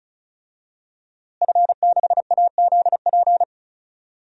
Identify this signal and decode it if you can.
Morse code, heard as a keyed tone, about 36 wpm, 700 Hz. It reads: F6AZP
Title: DXCC-100 Call Signs - 5 Characters - Repeated 2x - 35wpm